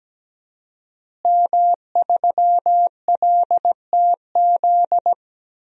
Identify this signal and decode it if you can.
Morse code, heard as a keyed tone, about 17 wpm, 700 Hz. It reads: M3LTZ
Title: DXCC-100 Call Signs - 5 Characters - Repeated 3x - 17wpm